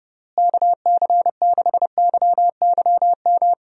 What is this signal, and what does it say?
Morse code, heard as a keyed tone, about 30 wpm, 700 Hz. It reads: KC6YYM
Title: USA Call Signs - 6 Characters - Repeated 2x - 30wpm